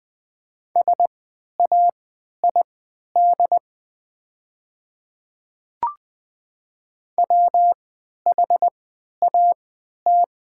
Morse code, heard as a keyed tone, about 20 wpm, spaced as for 11 wpm, 700 Hz. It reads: SAID WHAT
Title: 3x Character Spacing - Top 100 Words - 20wpm